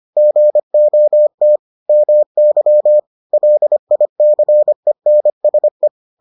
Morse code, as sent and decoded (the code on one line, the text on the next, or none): --. --- - / -- -.-- / .-.. .. -.-. -. ... .
GOT MY LICENSE